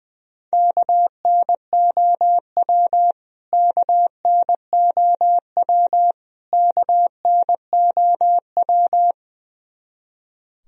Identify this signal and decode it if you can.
Morse code, heard as a keyed tone, about 21 wpm, 700 Hz. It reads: KNOW KNOW KNOW
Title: Top 100 Words Repeated 20wpm